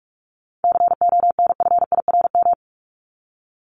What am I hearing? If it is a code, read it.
Morse code, heard as a keyed tone, about 44 wpm, 700 Hz. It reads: CONFIRM
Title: Top 1751-2000 Words 45wpm